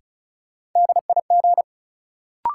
Morse code, heard as a keyed tone, about 36 wpm, 700 Hz. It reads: DIG
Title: Sentences from Top 1600 Words - New Word Drill - 35wpm